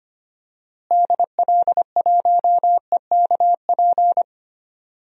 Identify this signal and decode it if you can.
Morse code, heard as a keyed tone, about 25 wpm, 700 Hz. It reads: DL1EKP
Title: DXCC-100 Call Signs - 6 Characters - Repeated 2x - 25wpm